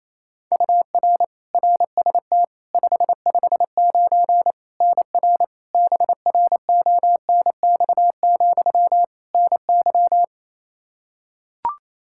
Morse code, heard as a keyed tone, about 28 wpm, 700 Hz. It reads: UR RST 559 NR BRONX, NY
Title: K1USN SST QSO Practice 28wpm